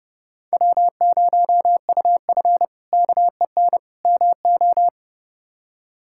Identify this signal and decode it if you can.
Morse code, heard as a keyed tone, about 30 wpm, 700 Hz. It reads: W0UF KEN MO
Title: K1USN SST QSO Practice - 30wpm